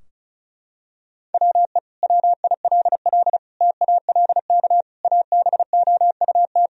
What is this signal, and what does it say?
Morse code, heard as a keyed tone, about 36 wpm, 700 Hz. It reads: WE WILL TALK ABOUT
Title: Sentences from Top 1000 Words - Review Part 4 - 35wpm